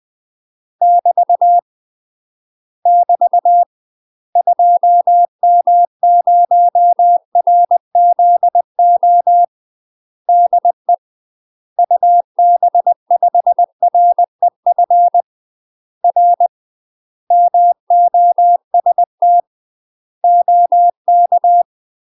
Morse code, heard as a keyed tone, about 20 wpm, 700 Hz. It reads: <BT> <BT> 2M0RZO DE UB5REF R MOST OK
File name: UK MASTER QSOs TRAINING / 20WPM / UK MASTER QSOs 0063